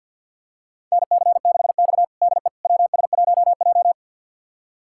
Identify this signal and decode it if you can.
Morse code, heard as a keyed tone, about 52 wpm, 700 Hz. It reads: NC6X DE WS1J
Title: USA Callsign DE Callsign - 4 Characters - 50wpm